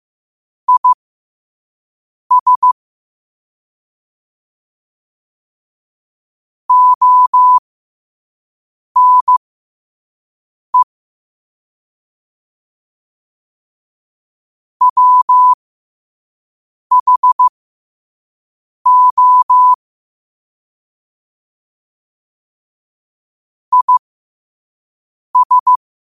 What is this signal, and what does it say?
Morse code, heard as a keyed tone, about 15 wpm, spaced as for 5 wpm, 1000 Hz. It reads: IS ONE WHO IS
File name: qotdfiles / 5wpm